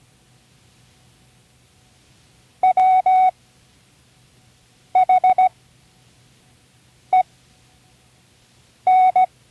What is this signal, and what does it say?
Morse code, heard as a keyed tone, about 17 wpm, spaced as for 5 wpm, 740 Hz. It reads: WHEN
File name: CWResources / Learning Morse Code (from Tune in the World) / 29 words using all letters